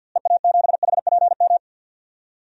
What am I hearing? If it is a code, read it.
Morse code, heard as a keyed tone, about 51 wpm, 700 Hz. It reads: EA7HPM
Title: DXCC-100 Call Signs - 6 Characters - Repeated 3x - 50wpm